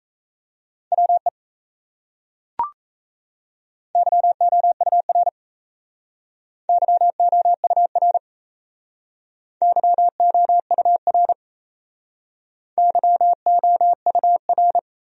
Morse code, heard as a keyed tone, about 37 wpm, 700 Hz. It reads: WE YOUR YOUR YOUR YOUR